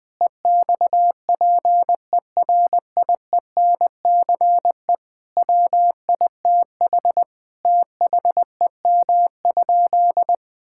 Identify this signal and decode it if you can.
Morse code, heard as a keyed tone, about 21 wpm, 700 Hz. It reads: EXPERIENCE WITH THEM?
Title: Sentences from Top 1000 Words - Review Part 5 - 20wpm